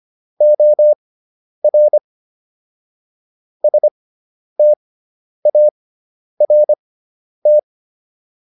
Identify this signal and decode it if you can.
Morse code, heard as a keyed tone, about 25 wpm, spaced as for 10 wpm, 600 Hz. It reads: OR START